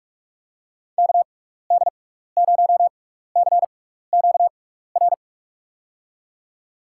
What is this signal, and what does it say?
Morse code, heard as a keyed tone, about 44 wpm, spaced as for 15 wpm, 700 Hz. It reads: KD0CQR
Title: USA Call Signs - 6x Character Spacing - 45wpm